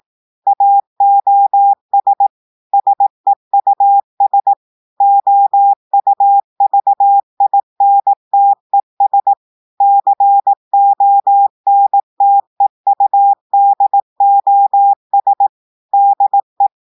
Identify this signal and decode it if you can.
Morse code, heard as a keyed tone, about 18 wpm, 800 Hz. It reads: AOS SEUS OUVINTES CONTEUDOS DE
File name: audios / portugais / portugais texte en clair 2 vitesse 18 mots minutes